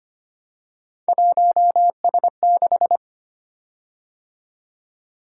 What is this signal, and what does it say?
Morse code, heard as a keyed tone, about 25 wpm, 700 Hz. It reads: 1S6
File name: Three Letter-Number Combinations - 25wpm